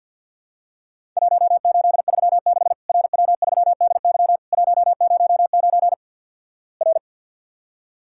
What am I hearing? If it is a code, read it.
Morse code, heard as a keyed tone, about 50 wpm, 700 Hz. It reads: 1836 RW3DY 109 R
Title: Worked All Europe DX Contest - QTC Exchange - 50wpm